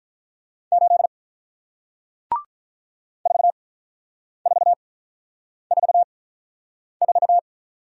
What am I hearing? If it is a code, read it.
Morse code, heard as a keyed tone, about 48 wpm, spaced as for 9 wpm, 700 Hz. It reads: 8 4444